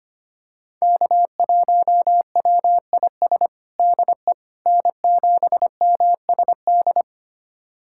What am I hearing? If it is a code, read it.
Morse code, heard as a keyed tone, about 25 wpm, 700 Hz. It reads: K1WIS DE N7MSD
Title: USA Callsign DE Callsign 25wpm